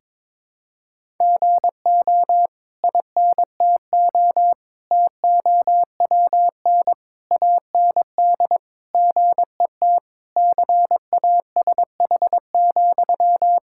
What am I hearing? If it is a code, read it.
Morse code, heard as a keyed tone, about 22 wpm, 700 Hz. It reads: GO INTO TOWN AND GET CASH,
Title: Sentences from Top 1000 Words - Review Part 6 - 22wpm